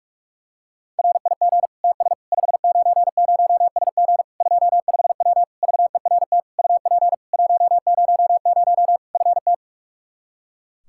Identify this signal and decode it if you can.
Morse code, heard as a keyed tone, about 46 wpm, 700 Hz. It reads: RIG TS 590SG 25W VERT UP 100 FT
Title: Practice QSO - Exchange 2 - Info - 45wpm